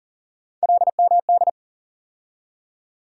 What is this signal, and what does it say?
Morse code, heard as a keyed tone, about 40 wpm, 700 Hz. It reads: LMD
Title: USA Call Sign Postfixes - 3 Letters 40wpm